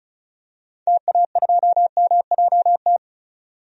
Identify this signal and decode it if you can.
Morse code, heard as a keyed tone, about 36 wpm, 700 Hz. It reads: TA2MJT
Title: DXCC-100 Call Signs - 6 Characters - 35wpm